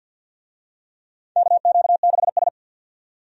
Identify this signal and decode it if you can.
Morse code, heard as a keyed tone, about 47 wpm, 700 Hz. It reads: KQ6S
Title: USA Call Signs - 4 Characters - 50wpm